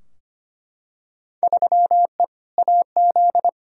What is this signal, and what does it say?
Morse code, heard as a keyed tone, about 25 wpm, 700 Hz. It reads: 3E AZ
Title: Field Day Exchanges - 25wpm